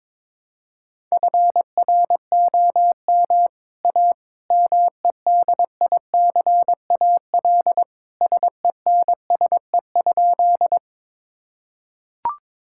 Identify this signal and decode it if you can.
Morse code, heard as a keyed tone, about 22 wpm, 700 Hz. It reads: FROM A MEDICAL SENSE?
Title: Sentences from Top 1000 Words - Review Part 5 - 22wpm